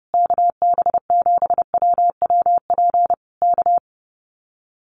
Morse code, heard as a keyed tone, about 30 wpm, 700 Hz. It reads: KB7WWP K
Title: Common QSO Phrases Encore 30wpm